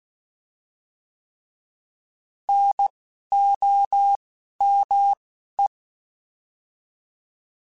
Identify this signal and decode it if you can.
Morse code, heard as a keyed tone, about 16 wpm, 800 Hz. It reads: N O M E